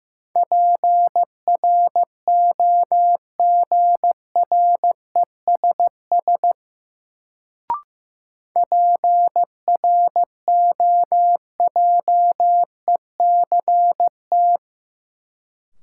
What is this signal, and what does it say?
Morse code, heard as a keyed tone, about 16 wpm, 700 Hz. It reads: PROGRESS PROJECT
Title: Sentences from Top 1400 Words - New Word Drill Alt - 15wpm